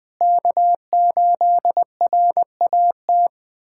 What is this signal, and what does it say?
Morse code, heard as a keyed tone, about 20 wpm, 700 Hz. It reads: K8RAT
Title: USA Callsign DE Callsign 20wpm